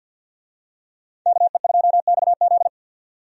Morse code, heard as a keyed tone, about 46 wpm, 700 Hz. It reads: KE2XZ
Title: USA Call Signs - 5 Characters - 50wpm